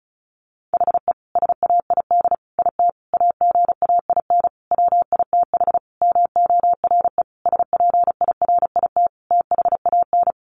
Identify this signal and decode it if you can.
Morse code, heard as a keyed tone, about 34 wpm, 700 Hz. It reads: HE SAID IT AGAIN WITH MORE SPIRIT THAN
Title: Sentences from Top 1400 Words 35wpm